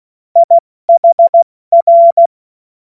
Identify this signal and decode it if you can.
Morse code, heard as a keyed tone, about 16 wpm, 680 Hz. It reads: IHR
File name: Morsekurs / Witze / Witz 02 April 2024 70 F80